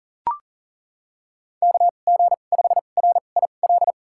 Morse code, heard as a keyed tone, about 40 wpm, spaced as for 29 wpm, 690 Hz. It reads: KG5RIL
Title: USA Call Signs - 2x Character Spacing - 40wpm